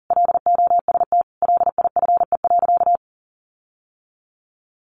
Morse code, heard as a keyed tone, about 40 wpm, 700 Hz. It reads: LOST LIFE.
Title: Sentences from Top 1400 Words 40wpm